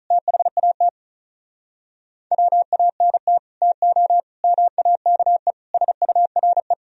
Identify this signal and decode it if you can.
Morse code, heard as a keyed tone, about 46 wpm, 700 Hz. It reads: THAT WANT TO MAKE SURE